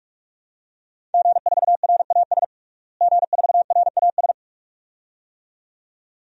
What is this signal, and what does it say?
Morse code, heard as a keyed tone, about 46 wpm, 700 Hz. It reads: G4RAS G4RAS
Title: DXCC-100 Call Signs - 5 Characters - Repeated 2x - 45wpm